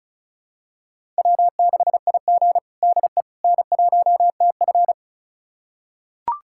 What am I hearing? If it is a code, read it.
Morse code, heard as a keyed tone, about 36 wpm, 700 Hz. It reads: W6IG DE N1TF
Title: USA Callsign DE Callsign - 4 Characters - 35wpm